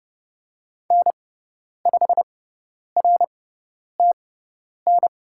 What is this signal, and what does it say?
Morse code, heard as a keyed tone, about 30 wpm, spaced as for 10 wpm, 700 Hz. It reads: N5RTN
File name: Lesson 17g - Letter u - Call Signs 30wpm